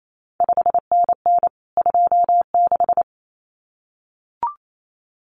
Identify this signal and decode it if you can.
Morse code, heard as a keyed tone, about 28 wpm, 700 Hz. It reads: 5NN 26
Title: CQ WW DX Contest Exchange Encore - 28wpm